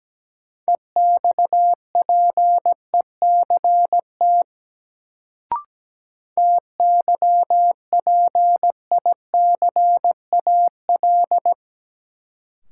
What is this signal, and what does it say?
Morse code, heard as a keyed tone, about 17 wpm, 700 Hz. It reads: EXPECT TYPICAL
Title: Sentences from Top 1100 Words - New Word Drill Alt - 17wpm